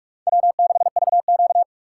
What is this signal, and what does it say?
Morse code, heard as a keyed tone, about 46 wpm, 700 Hz. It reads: W6VQ
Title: USA Call Signs - 4 Characters - 3x Character Spacing - 45wpm